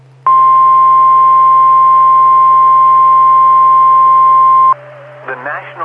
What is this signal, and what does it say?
Emergency Alert System (SAME) tones (the 1050 Hz attention tone), with speech. It is heard from 0.2 to 4.8 s.